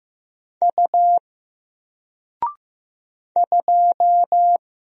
Morse code, heard as a keyed tone, about 15 wpm, 700 Hz. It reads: U 2
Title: Single Letter-Number 15wpm